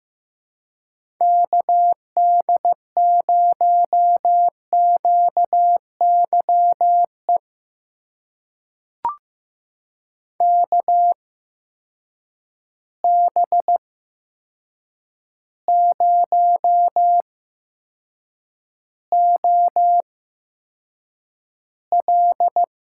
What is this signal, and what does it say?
Morse code, heard as a keyed tone, about 15 wpm, spaced as for 7 wpm, 700 Hz. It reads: KD0QYE KB0OL
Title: USA Call Signs - 6 Characters - 8x Character Spacing - 15wpm